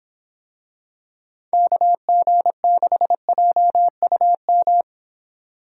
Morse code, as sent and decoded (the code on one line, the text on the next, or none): -.- --. -.... .--- ..- --
KG6JUM